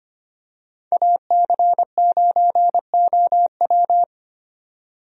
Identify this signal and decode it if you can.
Morse code, heard as a keyed tone, about 25 wpm, 700 Hz. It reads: AC9OW
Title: USA Call Signs - 5 Characters - 25wpm